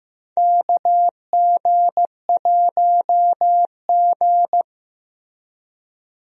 Morse code, heard as a keyed tone, about 16 wpm, 700 Hz. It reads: KG1G
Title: USA Call Signs - 4 Characters - Repeated 2x - 15wpm